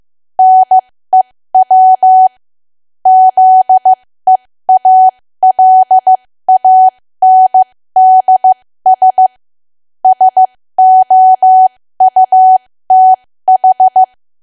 Morse code, heard as a keyed tone, about 15 wpm, spaced as for 13 wpm, 750 Hz. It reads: NEW ZEALANDS SOUTH
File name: cwa / W1AW-Level2 / 13WPM